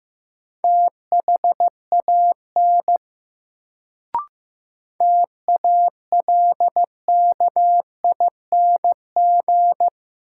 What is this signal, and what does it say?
Morse code, heard as a keyed tone, about 15 wpm, 700 Hz. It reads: THAN TALKING
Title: Sets of 2 Words - New Word Drill - 15wpm